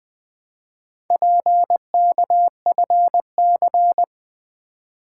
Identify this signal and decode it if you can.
Morse code, heard as a keyed tone, about 20 wpm, 700 Hz. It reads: PKFC
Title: Four Letter-Number Combinations - 20wpm